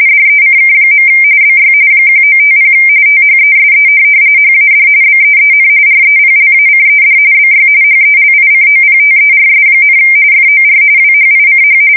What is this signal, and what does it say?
RTTY, about 45.45 baud. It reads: DECK. BY 1950 THESE HAD BEEN REPLACED BY HELICOPTERS, FOR WHICH
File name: fsk170 / USS-New-Jersey